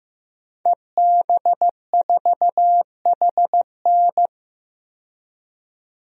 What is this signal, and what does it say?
Morse code, heard as a keyed tone, about 16 wpm, 700 Hz. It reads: EB4HN
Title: DXCC-100 Call Signs - 5 Characters - 15wpm